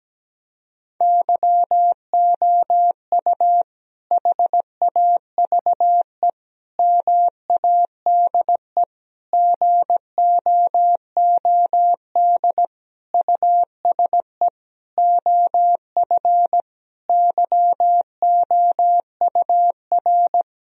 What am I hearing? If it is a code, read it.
Morse code, heard as a keyed tone, about 17 wpm, 700 Hz. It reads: YOU HAVE MADE GOOD USE OF YOUR
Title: Sentences from Top 1000 Words - Review Part 1 - 17wpm